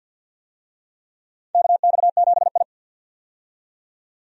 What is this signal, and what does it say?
Morse code, heard as a keyed tone, about 51 wpm, 700 Hz. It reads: KX7I
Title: USA Call Signs - 4 Characters - 50wpm